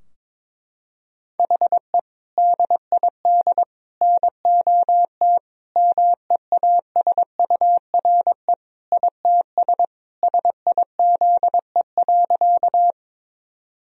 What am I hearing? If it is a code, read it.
Morse code, heard as a keyed tone, about 22 wpm, 700 Hz. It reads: HE DID NOT MEASURE ITS SIZE.
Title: Sentences from Top 1000 Words - Review Part 2 - 22wpm